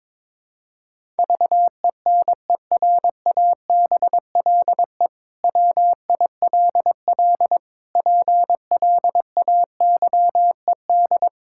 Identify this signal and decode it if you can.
Morse code, heard as a keyed tone, about 21 wpm, 700 Hz. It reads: VENERABLE WILL PLAYED
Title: Pangram Challenge v2 - 22wpm